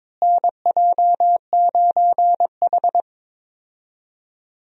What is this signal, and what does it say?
Morse code, heard as a keyed tone, about 22 wpm, 700 Hz. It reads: NJ9H
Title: USA Call Signs - 4 Characters - Repeated 3x - 22wpm